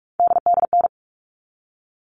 Morse code, heard as a keyed tone, about 44 wpm, 700 Hz. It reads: DDN
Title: All Cut Numbers - 3-Digit 45wpm